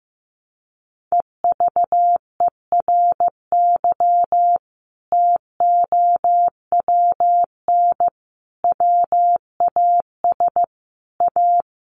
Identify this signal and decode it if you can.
Morse code, heard as a keyed tone, about 15 wpm, 700 Hz. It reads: EVERY TOWN WAS A
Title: Sentences from Top 1500 Words 15wpm